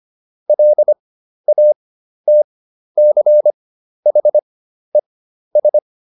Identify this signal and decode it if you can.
Morse code, heard as a keyed tone, about 25 wpm, spaced as for 12 wpm, 600 Hz. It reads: LATCHES